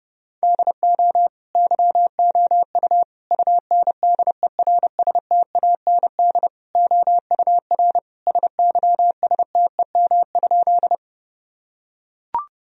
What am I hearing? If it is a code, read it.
Morse code, heard as a keyed tone, about 30 wpm, 700 Hz. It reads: DO YOU UNDERSTAND OUR SYSTEM?
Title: Sentences from Top 1000 Words - Review Part 3 - 30wpm